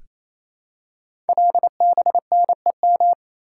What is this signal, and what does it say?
Morse code, heard as a keyed tone, about 27 wpm, 700 Hz. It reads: LBNEM
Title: Code Groups - 5 Letters - 28wpm